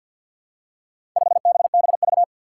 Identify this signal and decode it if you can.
Morse code, heard as a keyed tone, about 47 wpm, 700 Hz. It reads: 5BBV
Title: Four Challenging Letter-Number Combinations 50wpm